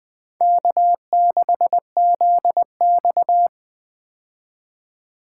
Morse code, heard as a keyed tone, about 20 wpm, 700 Hz. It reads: K6ZX
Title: K1USN SST QSO Practice - 20wpm